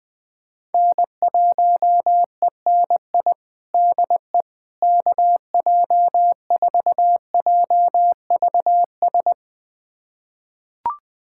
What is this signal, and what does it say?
Morse code, heard as a keyed tone, about 20 wpm, 700 Hz. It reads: N1ENI DE KJ4JVS
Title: USA Callsign DE Callsign 20wpm